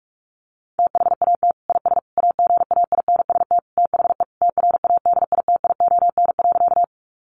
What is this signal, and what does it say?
Morse code, heard as a keyed tone, about 44 wpm, 700 Hz. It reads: THAT IS AGAINST THE TRADITION.
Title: Sentences from Top 1500 Words 45wpm